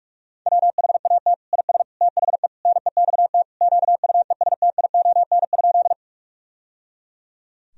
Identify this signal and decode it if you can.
Morse code, heard as a keyed tone, about 42 wpm, 700 Hz. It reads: WHAT IS THE NEXT QUESTION?
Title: Sentences from Top 1000 Words - Review Part 1 - 45wpm